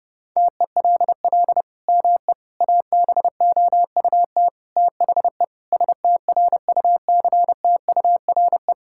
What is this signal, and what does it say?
Morse code, heard as a keyed tone, about 32 wpm, 700 Hz. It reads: TELL ME ABOUT THE STRUCTURE
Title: Sentences from Top 1000 Words - Review Part 6 - 30wpm